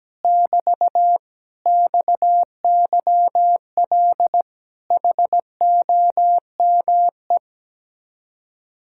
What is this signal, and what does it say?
Morse code, heard as a keyed tone, about 17 wpm, 700 Hz. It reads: <BT> XYL HOME
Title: Speed-Racing Common QSO Phrases Encore - 17wpm